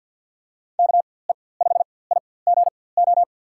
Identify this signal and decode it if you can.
Morse code, heard as a keyed tone, about 48 wpm, spaced as for 23 wpm, 700 Hz. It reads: KE5IGO
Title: USA Call Signs - 6 Characters - 4x Character Spacing - 50wpm